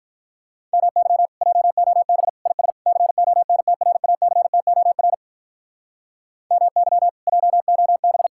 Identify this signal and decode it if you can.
Morse code, heard as a keyed tone, about 46 wpm, 700 Hz. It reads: MY JOB IS CONTRACTOR MY JOB